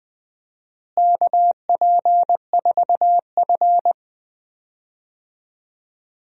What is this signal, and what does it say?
Morse code, heard as a keyed tone, about 19 wpm, 700 Hz. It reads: KP4F
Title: DXCC-100 Call Signs - 4 Characters - Repeated 2x - 20wpm